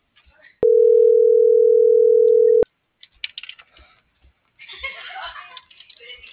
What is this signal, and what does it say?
Ringback tone (North America), 440 + 480 Hz.